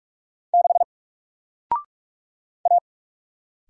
Morse code, heard as a keyed tone, about 44 wpm, 700 Hz. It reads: 6 A
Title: Single Letter-Number 45wpm